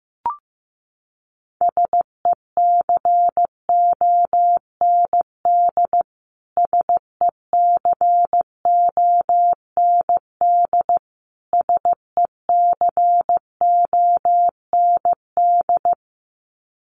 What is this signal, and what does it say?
Morse code, heard as a keyed tone, about 15 wpm, 700 Hz. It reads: SECOND SECOND SECOND
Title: Top 500 Words Repeated 15wpm